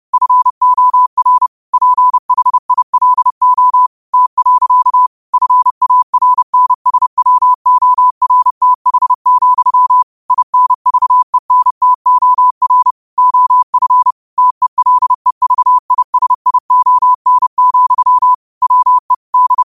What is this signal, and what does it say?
Morse code, heard as a keyed tone, about 30 wpm, 1000 Hz. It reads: FOR PHILO T. FARNSWORTH, INVENTOR OF TELEVISION, WE D